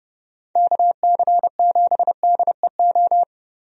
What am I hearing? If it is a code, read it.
Morse code, heard as a keyed tone, about 30 wpm, 700 Hz. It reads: KC7DEO
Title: USA Call Signs - 6 Characters - Repeated 2x - 30wpm